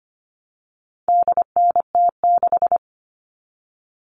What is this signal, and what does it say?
Morse code, heard as a keyed tone, about 25 wpm, 700 Hz. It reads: DNT6
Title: All Cut Numbers - 4-Digit 25wpm